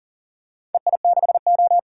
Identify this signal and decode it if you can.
Morse code, heard as a keyed tone, about 38 wpm, 700 Hz. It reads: EI6O
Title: DXCC-100 Call Signs - 4 Characters - 40wpm